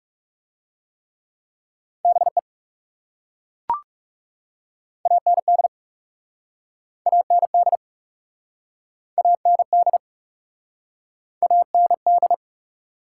Morse code, heard as a keyed tone, about 40 wpm, 700 Hz. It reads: BE AND AND AND AND